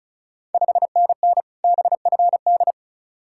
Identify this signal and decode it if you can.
Morse code, heard as a keyed tone, about 36 wpm, 700 Hz. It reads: 5NN BFD
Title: Ontario QSO Party Exchange 35wpm